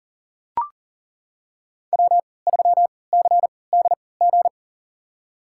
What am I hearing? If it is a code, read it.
Morse code, heard as a keyed tone, about 39 wpm, spaced as for 22 wpm, 700 Hz. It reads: W3CDG
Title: USA Call Signs - 3x Character Spacing - 40wpm